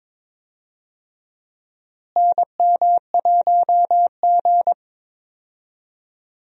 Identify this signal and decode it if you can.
Morse code, heard as a keyed tone, about 22 wpm, 700 Hz. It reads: NM1G
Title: USA Call Signs - 4 Characters - Repeated 2x - 22wpm